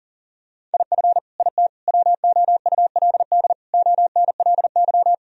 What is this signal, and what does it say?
Morse code, heard as a keyed tone, about 40 wpm, 700 Hz. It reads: IF IT WOULD ONLY
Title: Sentences from Top 1000 Words - Review Part 3 - 40wpm